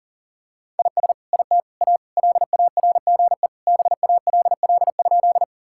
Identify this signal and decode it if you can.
Morse code, heard as a keyed tone, about 40 wpm, 700 Hz. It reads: IS IT A LARGE BALL?
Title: Sentences from Top 1000 Words - Review Part 6 - 40wpm